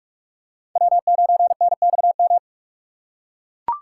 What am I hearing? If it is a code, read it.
Morse code, heard as a keyed tone, about 42 wpm, 700 Hz. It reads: W9NXM